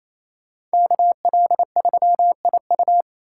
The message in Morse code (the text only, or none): KL3IU